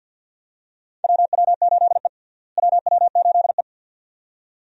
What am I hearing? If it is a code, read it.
Morse code, heard as a keyed tone, about 50 wpm, 700 Hz. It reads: WW8E WW8E
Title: USA Call Signs - 4 Characters - Repeated 2x - 50wpm